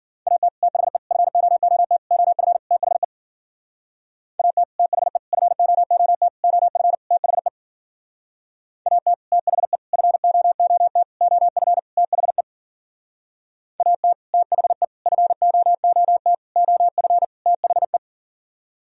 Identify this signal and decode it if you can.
Morse code, heard as a keyed tone, about 52 wpm, 700 Hz. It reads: AT THE FOOT OF THE AT THE FOOT OF THE AT THE FOOT OF THE AT THE FOOT OF THE